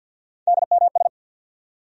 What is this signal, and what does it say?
Morse code, heard as a keyed tone, about 49 wpm, 700 Hz. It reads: DMS
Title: USA Call Sign Postfixes - 3 Letters 50wpm